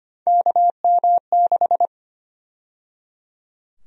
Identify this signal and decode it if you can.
Morse code, heard as a keyed tone, about 25 wpm, 700 Hz. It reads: KM6
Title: USA Call Sign Prefixes - 25wpm